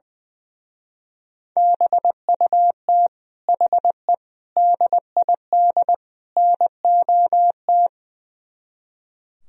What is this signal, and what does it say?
Morse code, heard as a keyed tone, about 20 wpm, 700 Hz. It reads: BUT HE DID NOT
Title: Speed-Racing - Sets of 4 Words 20wpm